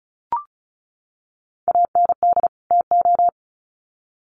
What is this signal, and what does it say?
Morse code, heard as a keyed tone, about 34 wpm, 700 Hz. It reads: AND TO
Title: Sets of 2 Words 35wpm